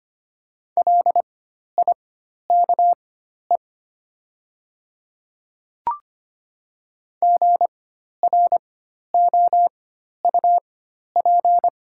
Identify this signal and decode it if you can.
Morse code, heard as a keyed tone, about 25 wpm, spaced as for 12 wpm, 700 Hz. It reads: LIKE GROUP
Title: 4x Character Spacing - Top 300 Words - 25wpm